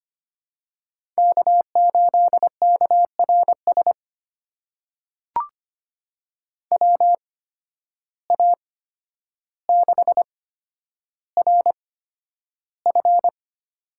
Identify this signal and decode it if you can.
Morse code, heard as a keyed tone, about 25 wpm, spaced as for 11 wpm, 700 Hz. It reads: K8KRS WA6RF
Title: USA Call Signs - 5 Characters - 8x Character Spacing - 25wpm